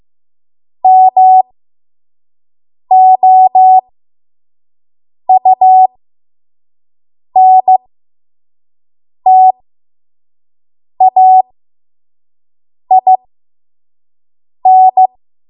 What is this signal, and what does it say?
Morse code, heard as a keyed tone, about 15 wpm, spaced as for 5 wpm, 750 Hz. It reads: MOUNTAIN
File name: Morse / 05WPM